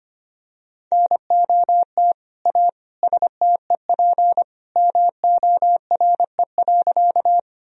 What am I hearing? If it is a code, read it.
Morse code, heard as a keyed tone, about 25 wpm, 700 Hz. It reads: NOT A STEP MORE.
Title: Sentences from Top 1000 Words - Review Part 2 - 25wpm